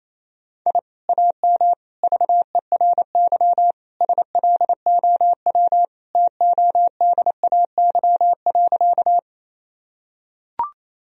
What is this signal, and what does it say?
Morse code, heard as a keyed tone, about 28 wpm, 700 Hz. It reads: I AM VERY SLOW TODAY.
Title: Sentences from Top 1000 Words - Review Part 2 - 28wpm